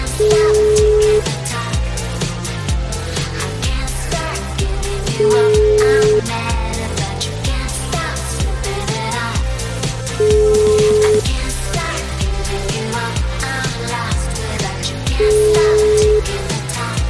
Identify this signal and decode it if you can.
Ringback tone (Europe, 425 Hz), with noise.